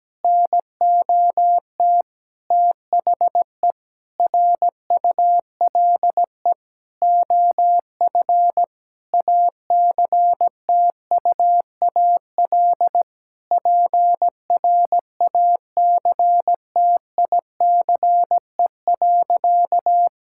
Morse code, heard as a keyed tone, about 17 wpm, 700 Hz. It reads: NOT THE RULE OF ACTUAL PRACTICE.
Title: Sentences from Top 1000 Words - Review Part 5 - 17wpm